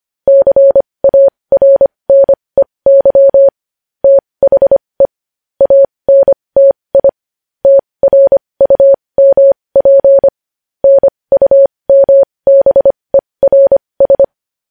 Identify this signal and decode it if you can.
Morse code, heard as a keyed tone, about 25 wpm, spaced as for 20 wpm, 550 Hz. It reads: CARNEY THE ANTI TRUMP NUMBERS
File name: podcast / News headlines-1x-25x20